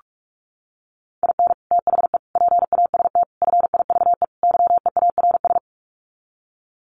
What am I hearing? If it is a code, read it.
Morse code, heard as a keyed tone, about 44 wpm, 700 Hz. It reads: IN THE PAST FIVE YEARS
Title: Sets of 5 Words Encore 45wpm